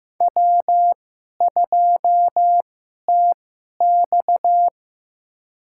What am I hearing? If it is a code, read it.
Morse code, heard as a keyed tone, about 15 wpm, spaced as for 11 wpm, 700 Hz. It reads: W2TX
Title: USA Call Signs - 4 Characters - 2x Character Spacing - 15wpm